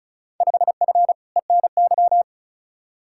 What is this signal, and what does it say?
Morse code, heard as a keyed tone, about 36 wpm, 700 Hz. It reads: 5F ENY
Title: Field Day Exchanges Encore - 35wpm